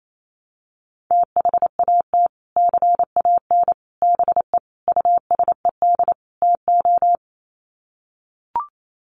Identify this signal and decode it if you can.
Morse code, heard as a keyed tone, about 28 wpm, 700 Hz. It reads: THAT CAN BE USED TO
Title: Sets of 5 Words Encore 28wpm